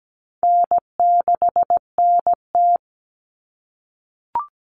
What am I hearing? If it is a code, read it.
Morse code, heard as a keyed tone, about 17 wpm, 700 Hz. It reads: N6NT
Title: Common Cut Numbers - 4-Digit 17wpm